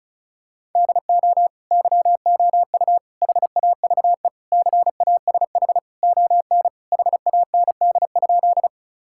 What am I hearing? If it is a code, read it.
Morse code, heard as a keyed tone, about 34 wpm, 700 Hz. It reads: DO YOU HAVE CASH ON HAND?
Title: Sentences from Top 1000 Words - Review Part 5 - 35wpm